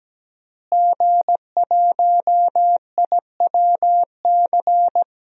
Morse code, heard as a keyed tone, about 17 wpm, 700 Hz. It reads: G1IWC
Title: DXCC-100 Call Signs - 5 Characters - Repeated 3x - 17wpm